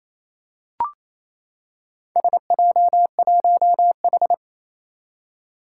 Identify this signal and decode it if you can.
Morse code, heard as a keyed tone, about 29 wpm, 700 Hz. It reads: SJ1H
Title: Four Challenging Letter-Number Combinations 28wpm